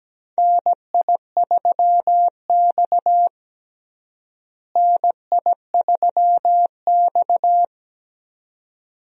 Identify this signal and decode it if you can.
Morse code, heard as a keyed tone, about 17 wpm, 700 Hz. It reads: NI3X NI3X
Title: USA Call Signs - 4 Characters - Repeated 2x - 17wpm